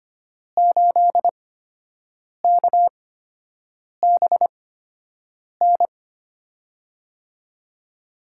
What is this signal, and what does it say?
Morse code, heard as a keyed tone, about 25 wpm, spaced as for 7 wpm, 700 Hz. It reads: 8KBN
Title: USA Call Signs - 6 Characters - 8x Character Spacing - 25wpm